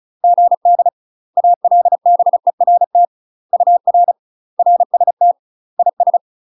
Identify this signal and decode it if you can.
Morse code, heard as a keyed tone, about 35 wpm, 700 Hz. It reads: GD ALBERT UR RST IS
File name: UK MASTER QSOs TRAINING / 35WPM / UK MASTER QSOs 0014